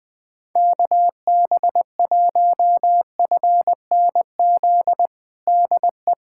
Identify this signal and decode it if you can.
Morse code, heard as a keyed tone, about 20 wpm, 700 Hz. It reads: KB1FNZ DE
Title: USA Callsign DE Callsign - 6 Characters - 20wpm